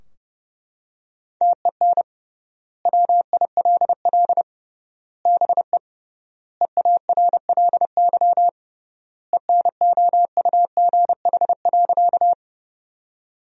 Morse code, heard as a keyed tone, about 30 wpm, 700 Hz. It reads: TEN WILL BE EARLY ENOUGH.
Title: 3x Word-Spacing - Sentences from Top 400 Words - 30wpm